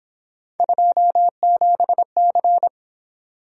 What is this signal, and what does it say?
Morse code, heard as a keyed tone, about 26 wpm, 700 Hz. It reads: 27C
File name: Speed-Racing - Three Letter-Number Combinations - 22wpm